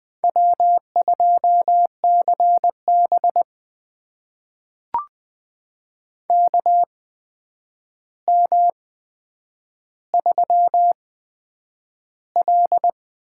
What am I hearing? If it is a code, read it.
Morse code, heard as a keyed tone, about 20 wpm, spaced as for 9 wpm, 700 Hz. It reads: W2CB KM3L
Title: USA Call Signs - 8x Character Spacing - 20wpm